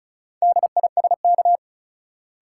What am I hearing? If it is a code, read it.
Morse code, heard as a keyed tone, about 34 wpm, 700 Hz. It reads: DISK
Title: Sentences from Top 1000 Words - Review Part 6 - New Word Drill - 35wpm